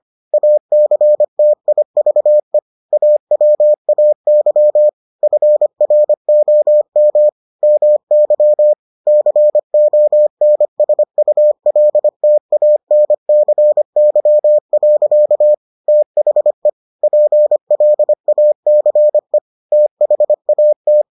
Morse code, heard as a keyed tone, about 25 wpm, 600 Hz. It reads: ACTIVE AWAY FROM MY CONSULTANCY. THE PLACE THAT